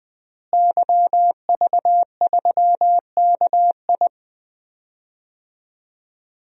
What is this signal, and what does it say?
Morse code, heard as a keyed tone, about 21 wpm, 700 Hz. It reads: YV3KI
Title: DXCC-100 Call Signs - 5 Characters - Repeated 2x - 20wpm